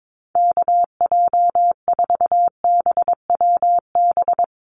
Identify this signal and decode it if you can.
Morse code, heard as a keyed tone, about 22 wpm, 700 Hz. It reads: KJ4BWB
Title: USA Call Signs 22wpm